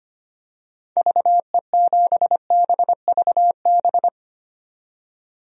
Morse code, heard as a keyed tone, about 25 wpm, 700 Hz. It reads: VE7BVB
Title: DXCC-100 Call Signs - 6 Characters - 25wpm